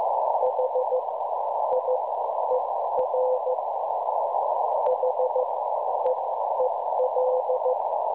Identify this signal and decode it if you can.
Morse code, heard as a keyed tone, about 14 wpm, spaced as for 10 wpm, 510 Hz, with noise. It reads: HIER HEEL
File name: cwronde / websdr recording .6kHz